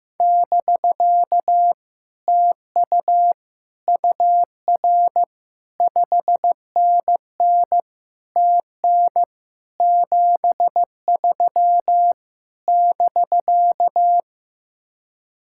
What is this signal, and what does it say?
Morse code, heard as a keyed tone, about 15 wpm, 700 Hz. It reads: <BK> TU UR 5NN TN 73 <BK>
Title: Speed-Racing Common QSO Phrases Encore - 15wpm